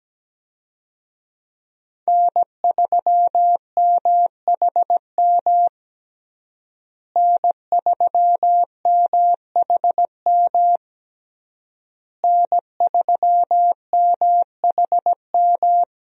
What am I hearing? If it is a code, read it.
Morse code, heard as a keyed tone, about 17 wpm, 700 Hz. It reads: N3MHM N3MHM N3MHM
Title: USA Call Signs - 5 Characters - Repeated 3x - 17wpm